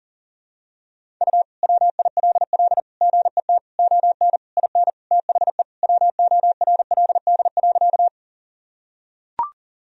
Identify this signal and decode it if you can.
Morse code, heard as a keyed tone, about 40 wpm, 700 Hz. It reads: U WILL GET ON IN THE WORLD.
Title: Sentences from Top 200 Words and Abbreviations 40wpm